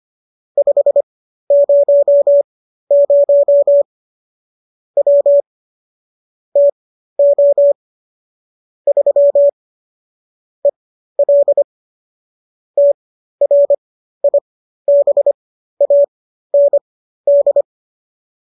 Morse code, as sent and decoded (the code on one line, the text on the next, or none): ..... ----- ----- / .-- / - --- / ...-- / . .-.. / - .-. .. -... .- -. -..
500 W TO 3 EL TRIBAND